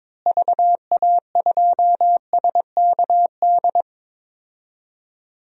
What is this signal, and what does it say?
Morse code, heard as a keyed tone, about 22 wpm, 700 Hz. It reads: VA2SKD
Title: DXCC-100 Call Signs - 6 Characters - Repeated 2x - 22wpm